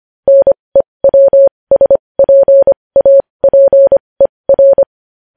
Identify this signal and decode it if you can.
Morse code, heard as a keyed tone, about 25 wpm, spaced as for 20 wpm, 550 Hz. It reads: NEWSPAPER
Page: News Headlines in Morse Code at 20 WPM